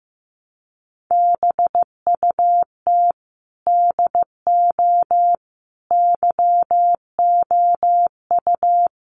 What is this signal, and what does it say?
Morse code, heard as a keyed tone, about 15 wpm, 700 Hz. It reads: BUT DO YOU
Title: Sentences from Top 600 Words 15wpm